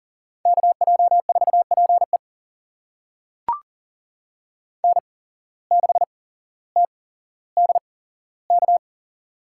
Morse code, heard as a keyed tone, about 39 wpm, spaced as for 17 wpm, 700 Hz. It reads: KJ4PE N6TDK
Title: USA Call Signs - 5 Characters - 8x Character Spacing - 40wpm